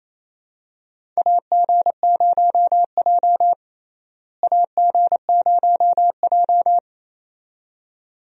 Morse code, heard as a keyed tone, about 28 wpm, 700 Hz. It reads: AG0J AG0J
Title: USA Call Signs - 4 Characters - Repeated 2x - 28wpm